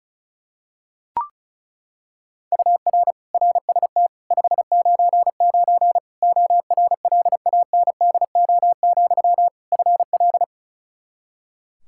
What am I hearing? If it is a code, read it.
Morse code, heard as a keyed tone, about 34 wpm, 700 Hz. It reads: UR RST 599 ORLANDO, FL
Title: Practice QSO - Exchange 1 - Beginning - 35wpm